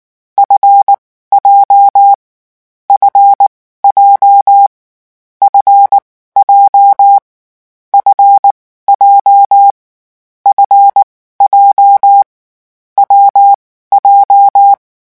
Morse code, heard as a keyed tone, about 19 wpm, spaced as for 14 wpm, 800 Hz. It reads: FJ FJ F J F J F J W J
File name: morse mp3 / lecon7-vitesse 18